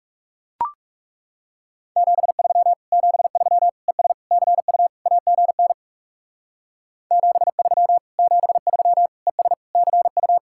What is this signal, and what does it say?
Morse code, heard as a keyed tone, about 35 wpm, 700 Hz. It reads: 73 73 ES CU AGN 73 73 ES CU
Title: Speed-Racing Common QSO Phrases - 30wpm